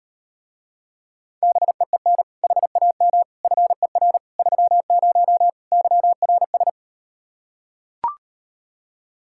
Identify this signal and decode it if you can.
Morse code, heard as a keyed tone, about 39 wpm, 700 Hz. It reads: BEEN HAM FER 30 YRS